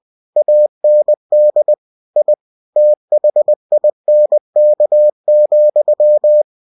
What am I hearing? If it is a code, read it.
Morse code, heard as a keyed tone, about 20 wpm, 600 Hz. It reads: AND I THINK,